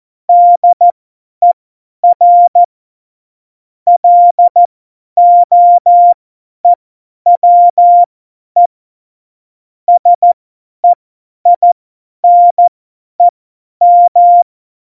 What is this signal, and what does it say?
Morse code, heard as a keyed tone, about 14 wpm, spaced as for 10 wpm, 700 Hz. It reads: DER LOEWE SEINEM
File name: Morsekurs / Witze / Witz 17.Juli 2018, 50 F70